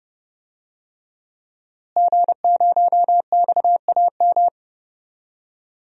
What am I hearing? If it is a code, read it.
Morse code, heard as a keyed tone, about 30 wpm, 700 Hz. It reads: G0XAM
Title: DXCC-100 Call Signs - 5 Characters - Repeated 3x - 30wpm